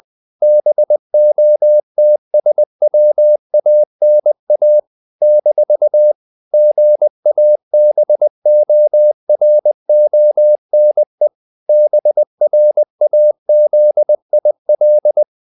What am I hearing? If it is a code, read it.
Morse code, heard as a keyed tone, about 20 wpm, 600 Hz. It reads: BOTSWANA - GABORONE BRAZIL
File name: CW / Practice / NationalCapitals20